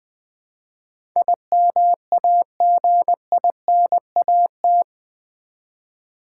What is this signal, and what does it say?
Morse code, heard as a keyed tone, about 20 wpm, 700 Hz. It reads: IMAGINAT
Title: Build A Word - 4 Letters At A Time - 11-Letter Words - 20wpm